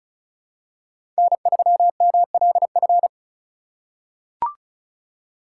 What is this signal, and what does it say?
Morse code, heard as a keyed tone, about 36 wpm, 700 Hz. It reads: N3MLF
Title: USA Call Signs - 5 Characters - 2x Character Spacing - 35wpm